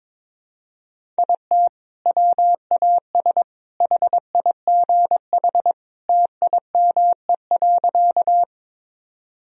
Morse code, heard as a keyed tone, about 22 wpm, 700 Hz. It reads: IT WAS HIGH TIME.
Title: Sentences from Top 1000 Words - Review Part 1 - 22wpm